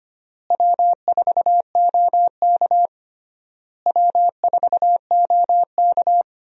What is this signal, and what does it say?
Morse code, heard as a keyed tone, about 25 wpm, 700 Hz. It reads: W4OK W4OK
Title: USA Call Signs - 4 Characters - Repeated 3x - 25wpm